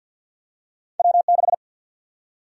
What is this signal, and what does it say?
Morse code, heard as a keyed tone, about 53 wpm, 700 Hz. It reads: W6
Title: USA Call Sign Prefixes - 50wpm